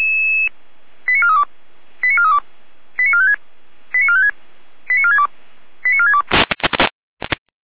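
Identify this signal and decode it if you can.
Five-tone selective call (ZVEI), heard as quick runs of five tones, tones of about 70 ms. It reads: 89432 89432 89456 89456 89462 89462
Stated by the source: ZVEI five-tone selective call 89401, received via SDR.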